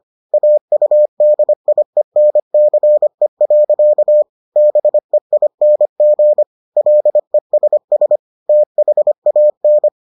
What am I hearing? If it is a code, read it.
Morse code, heard as a keyed tone, about 25 wpm, 600 Hz. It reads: AUDIENCE. BEING LESS THAN